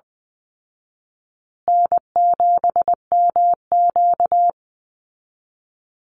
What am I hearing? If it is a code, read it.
Morse code, heard as a keyed tone, about 20 wpm, 700 Hz. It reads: N7MQ
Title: Easy Call Signs Repeated 3x - 20wpm